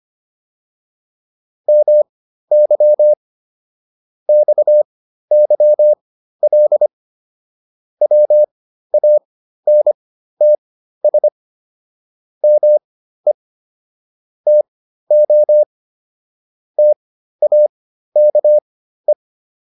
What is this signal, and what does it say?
Morse code, heard as a keyed tone, about 25 wpm, spaced as for 13 wpm, 600 Hz. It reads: MY XYL WANTS ME TO TAKE